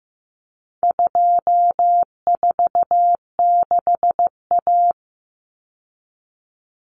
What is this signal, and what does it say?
Morse code, heard as a keyed tone, about 15 wpm, 700 Hz. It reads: 246A
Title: Common Cut Numbers - 4-Digit 15wpm